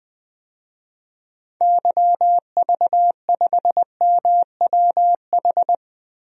Morse code, heard as a keyed tone, about 20 wpm, 700 Hz. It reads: YV5MWH
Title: DXCC-100 Call Signs - 6 Characters - Repeated 2x - 20wpm